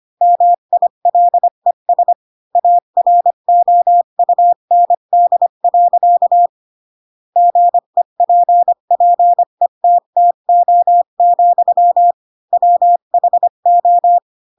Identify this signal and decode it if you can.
Morse code, heard as a keyed tone, about 25 wpm, 700 Hz. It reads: MILES AROUND. GEPPETTO, WHO
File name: Pinocchio / Pinocchio ch7 20 25 ik1hge